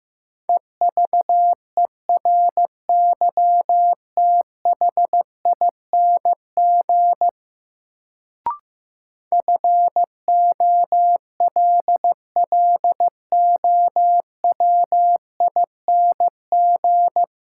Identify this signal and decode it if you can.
Morse code, heard as a keyed tone, about 15 wpm, 700 Hz. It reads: EVERYTHING FOLLOWING
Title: Sentences from Top 700 Words - New Word Drill - 15wpm